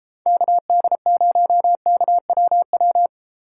Morse code, heard as a keyed tone, about 30 wpm, 700 Hz. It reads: KD0KWW